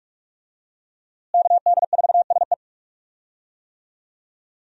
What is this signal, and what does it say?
Morse code, heard as a keyed tone, about 45 wpm, 700 Hz. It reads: KD4SE
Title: USA Call Signs - 5 Characters - Repeated 2x - 45wpm